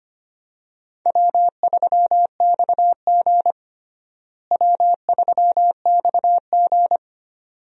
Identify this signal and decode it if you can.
Morse code, heard as a keyed tone, about 25 wpm, 700 Hz. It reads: W3XG W3XG
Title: USA Call Signs - 4 Characters - Repeated 2x - 25wpm